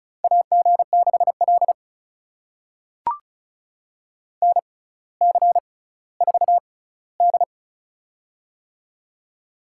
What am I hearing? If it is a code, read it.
Morse code, heard as a keyed tone, about 36 wpm, spaced as for 18 wpm, 700 Hz. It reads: AG6L NC4D
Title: USA Call Signs - 4 Characters - 6x Character Spacing - 35wpm